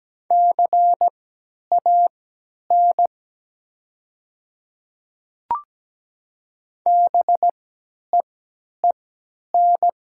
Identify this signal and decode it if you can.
Morse code, heard as a keyed tone, about 17 wpm, spaced as for 10 wpm, 700 Hz. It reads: CAN BEEN
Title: 3x Character Spacing - Top 100 Words - 17wpm